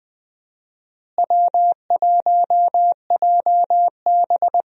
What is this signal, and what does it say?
Morse code, heard as a keyed tone, about 20 wpm, 700 Hz. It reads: W1JB
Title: USA Call Signs - 4 Characters - Repeated 2x - 20wpm